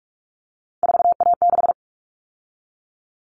Morse code, heard as a keyed tone, about 44 wpm, 700 Hz. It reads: 4A6
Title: All Cut Numbers - 3-Digit 45wpm